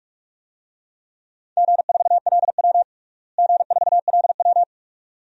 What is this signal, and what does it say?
Morse code, heard as a keyed tone, about 41 wpm, 700 Hz. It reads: G4LW G4LW
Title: DXCC-100 Call Signs - 4 Characters - Repeated 2x - 45wpm